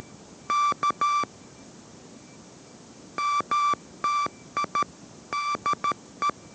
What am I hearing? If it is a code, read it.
Morse code, heard as a keyed tone, about 13 wpm, 1170 Hz. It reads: K MTI DE